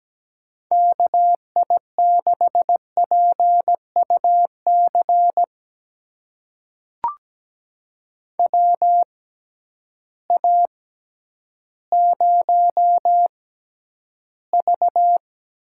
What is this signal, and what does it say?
Morse code, heard as a keyed tone, about 17 wpm, spaced as for 10 wpm, 700 Hz. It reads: KI6PUC WA0V
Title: USA Call Signs - 6 Characters - 6x Character Spacing - 17wpm